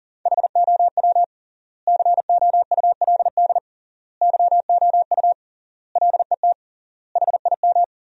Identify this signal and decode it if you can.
Morse code, heard as a keyed tone, about 37 wpm, 700 Hz. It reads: HOW COULD YOU LET HIM
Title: 3x Word-Spacing - Sentences from Top 300 Words - 40wpm